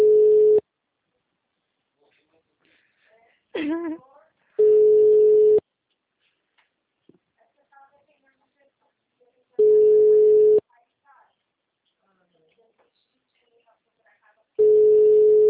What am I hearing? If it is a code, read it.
Ringback tone (Europe, 425 Hz), with speech.